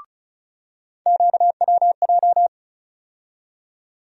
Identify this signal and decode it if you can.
Morse code, heard as a keyed tone, about 34 wpm, 700 Hz. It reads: QWJ
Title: USA Call Sign Postfixes - 3 Letters 35wpm